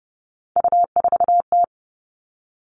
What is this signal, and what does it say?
Morse code, heard as a keyed tone, about 30 wpm, 700 Hz. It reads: U4T
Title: All Cut Numbers - 3-Digit 30wpm